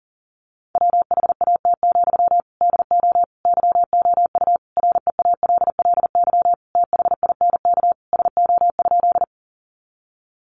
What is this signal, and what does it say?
Morse code, heard as a keyed tone, about 40 wpm, 700 Hz. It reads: WHAT, DO YOU REALLY THINK SO?
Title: Sentences from Top 600 Words 40wpm